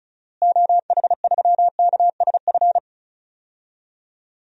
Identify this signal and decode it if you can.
Morse code, heard as a keyed tone, about 36 wpm, 700 Hz. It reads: OH3KSF
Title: DXCC-100 Call Signs - 6 Characters - Repeated 2x - 35wpm